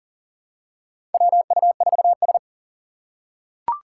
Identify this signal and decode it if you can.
Morse code, heard as a keyed tone, about 40 wpm, 700 Hz. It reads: WU4S
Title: USA Call Signs - 4 Characters - 3x Character Spacing - 40wpm